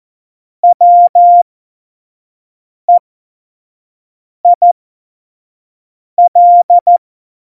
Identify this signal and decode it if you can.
Morse code, heard as a keyed tone, about 14 wpm, spaced as for 5 wpm, 700 Hz. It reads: WEIL
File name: Morsekurs / Witze / Witz 18 August 2020, 25 F70